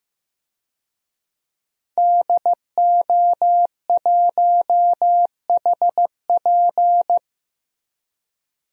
Morse code, heard as a keyed tone, about 15 wpm, 700 Hz. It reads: DO1HP
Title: DXCC-100 Call Signs - 5 Characters - Repeated 2x - 15wpm